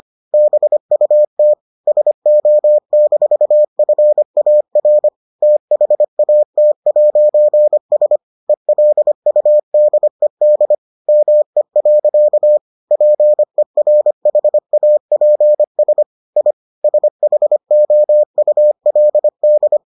Morse code, heard as a keyed tone, about 25 wpm, 600 Hz. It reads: BUT SO-FAR THAT'S ELUDED ME. PERHAPS I SHOULD